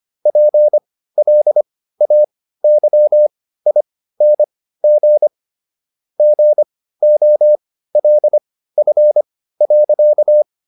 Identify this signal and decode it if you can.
Morse code, heard as a keyed tone, about 25 wpm, spaced as for 15 wpm, 600 Hz. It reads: PLAYING GOLF.